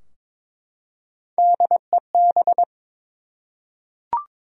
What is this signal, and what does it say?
Morse code, heard as a keyed tone, about 23 wpm, 700 Hz. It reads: DEB
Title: USA Common Names - Max 5 Letters - 22wpm